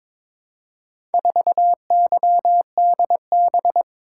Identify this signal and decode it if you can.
Morse code, heard as a keyed tone, about 22 wpm, 700 Hz. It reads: 4YDB
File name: practice / 22wpm / Four Letter-Number Combinations - 22wpm